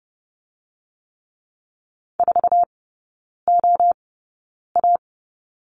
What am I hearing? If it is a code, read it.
Morse code, heard as a keyed tone, about 30 wpm, spaced as for 9 wpm, 690 Hz. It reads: 4OA
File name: Lesson 17e - Letter u - Three Characters 30wpm